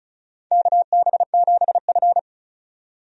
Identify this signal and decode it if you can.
Morse code, heard as a keyed tone, about 36 wpm, 700 Hz. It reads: KB7F
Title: USA Call Signs - 4 Characters - 35wpm